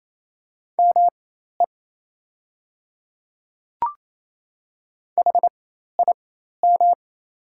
Morse code, heard as a keyed tone, about 28 wpm, spaced as for 13 wpm, 700 Hz. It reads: ME HIM
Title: 4x Character Spacing - Top 200 Words - 28wpm